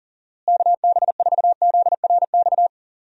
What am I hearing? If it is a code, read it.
Morse code, heard as a keyed tone, about 37 wpm, 700 Hz. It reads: KB4ZRX
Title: USA Call Signs - 6 Characters - Repeated 2x - 40wpm